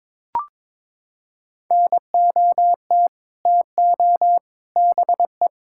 Morse code, heard as a keyed tone, about 22 wpm, 700 Hz. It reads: NOT TO BE
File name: practice / 22wpm / Sets of 6 Words - 22wpm